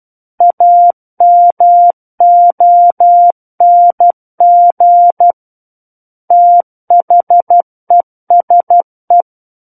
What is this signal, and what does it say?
Morse code, heard as a keyed tone, about 12 wpm, 700 Hz. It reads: AMONG THESE
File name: Book One / 03-Chapter 3 - 12 WPM